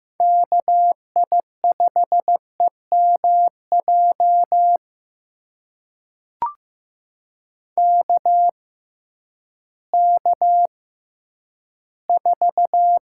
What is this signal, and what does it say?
Morse code, heard as a keyed tone, about 15 wpm, spaced as for 10 wpm, 700 Hz. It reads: KI5EMJ KK4
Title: USA Call Signs - 6 Characters - 6x Character Spacing - 15wpm